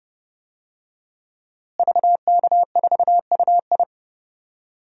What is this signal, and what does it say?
Morse code, heard as a keyed tone, about 30 wpm, 700 Hz. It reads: VK4UI
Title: DXCC-100 Call Signs - 5 Characters - Repeated 2x - 30wpm